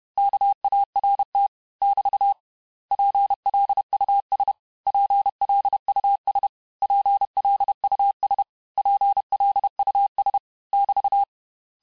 Morse code, heard as a keyed tone, about 31 wpm, 800 Hz. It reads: KART <BT> PLUS PLUS PLUS PLUS <BT>
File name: cours-du-mercredi / audios / cours-861-texte-les-mots-vitesse-29MPM